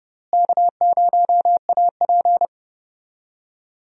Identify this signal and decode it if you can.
Morse code, heard as a keyed tone, about 30 wpm, 700 Hz. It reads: K0AP
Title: USA Call Signs - 4 Characters - Repeated 2x - 30wpm